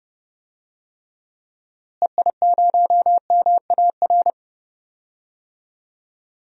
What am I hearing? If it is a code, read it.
Morse code, heard as a keyed tone, about 31 wpm, 700 Hz. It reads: EI0MAR
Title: DXCC-100 Call Signs - 6 Characters - Repeated 3x - 30wpm